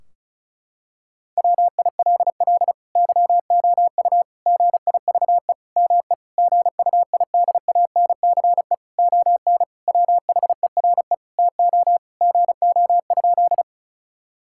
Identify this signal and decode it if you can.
Morse code, heard as a keyed tone, about 34 wpm, 700 Hz. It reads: WILL YOU GIVE ME GUIDANCE ON WHERE TO GO?
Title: Sentences from Top 1600 Words 35wpm